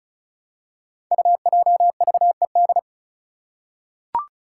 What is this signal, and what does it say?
Morse code, heard as a keyed tone, about 36 wpm, 700 Hz. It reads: UJVED
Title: Five Challenging Letter-Number Combinations 35wpm